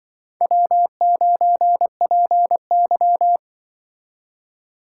Morse code, heard as a keyed tone, about 25 wpm, 700 Hz. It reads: W9PY
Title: Speed Racing - USA 1x2 Call Signs 20wpm